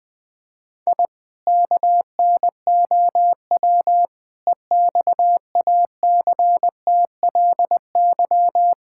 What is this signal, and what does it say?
Morse code, heard as a keyed tone, about 20 wpm, 700 Hz. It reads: I KNOW EXACTLY
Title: Sentences from Top 1000 Words - Review Part 6 - 20wpm